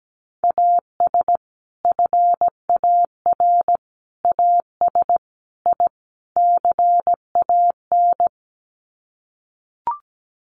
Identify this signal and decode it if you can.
Morse code, heard as a keyed tone, about 17 wpm, 700 Hz. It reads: AS FAR AS I CAN
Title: Sets of 5 Words Encore 17wpm